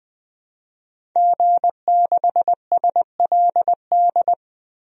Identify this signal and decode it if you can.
Morse code, heard as a keyed tone, about 20 wpm, 700 Hz. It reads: G6SLD
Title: DXCC-100 Call Signs - 5 Characters - Repeated 3x - 20wpm